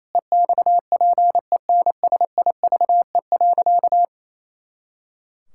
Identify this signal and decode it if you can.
Morse code, heard as a keyed tone, about 27 wpm, 700 Hz. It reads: EXPENSIVE.
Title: Pangram Challenge v2 - 28wpm